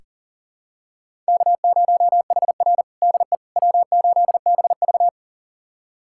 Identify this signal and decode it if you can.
Morse code, heard as a keyed tone, about 40 wpm, 700 Hz. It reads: K0HR DE W8BV
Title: USA Callsign DE Callsign - 4 Characters - 40wpm